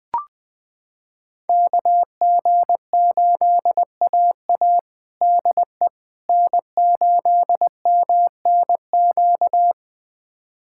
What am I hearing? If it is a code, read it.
Morse code, heard as a keyed tone, about 20 wpm, 700 Hz. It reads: KG8AA DE N8MNQ
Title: USA Callsign DE Callsign - 5 Characters - 20wpm